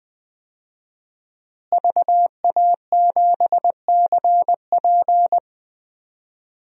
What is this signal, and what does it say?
Morse code, heard as a keyed tone, about 20 wpm, 700 Hz. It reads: VA7CP
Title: DXCC-100 Call Signs - 5 Characters - Repeated 3x - 20wpm